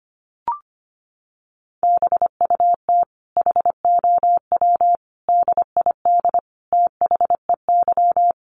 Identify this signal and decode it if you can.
Morse code, heard as a keyed tone, about 25 wpm, 700 Hz. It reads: BUT HOW DID THEY
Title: Sentences from Top 900 Words 25wpm